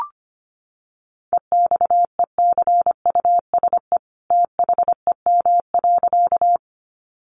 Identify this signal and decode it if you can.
Morse code, heard as a keyed tone, about 25 wpm, 700 Hz. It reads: EXECUSE THEM.
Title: Sentences from Top 700 Words 25wpm